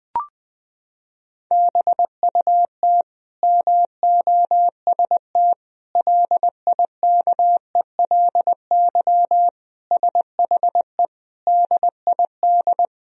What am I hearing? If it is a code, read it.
Morse code, heard as a keyed tone, about 20 wpm, 700 Hz. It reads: BUT MOST LIKELY SHE DID
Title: Sentences from Top 1000 Words - Review Part 4 - 20wpm